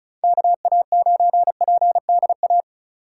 Morse code, heard as a keyed tone, about 34 wpm, 700 Hz. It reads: KA9PDA
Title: USA Call Signs - 4x Character Spacing - 35wpm